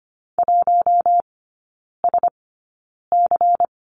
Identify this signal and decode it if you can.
Morse code, heard as a keyed tone, about 25 wpm, spaced as for 9 wpm, 700 Hz. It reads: 1SC
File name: Lesson 17e - Letter u - Three Characters 25wpm